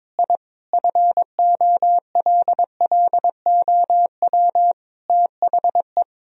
Morse code, heard as a keyed tone, about 22 wpm, 700 Hz. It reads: I FOLLOW THE
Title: Sentences from Top 1000 Words - Review Part 4 - 22wpm